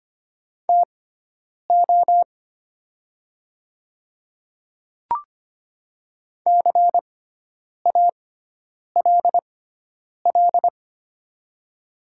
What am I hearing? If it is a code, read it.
Morse code, heard as a keyed tone, about 25 wpm, spaced as for 9 wpm, 700 Hz. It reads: TO CALL
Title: 6x Character Spacing - Top 100 Words - 25wpm